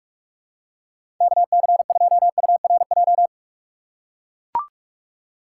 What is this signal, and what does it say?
Morse code, heard as a keyed tone, about 42 wpm, 700 Hz. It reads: KC2URJ
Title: USA Call Signs - 6 Characters - 8x Character Spacing - 45wpm